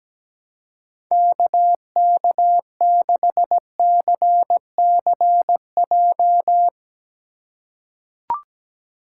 Morse code, heard as a keyed tone, about 17 wpm, 700 Hz. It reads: KK6CCJ
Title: USA Call Signs - 4x Character Spacing - 17wpm